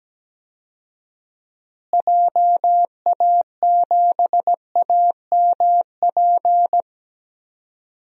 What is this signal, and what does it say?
Morse code, heard as a keyed tone, about 17 wpm, 700 Hz. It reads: JA7AMP
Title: DXCC-100 Call Signs - 6 Characters - Repeated 3x - 17wpm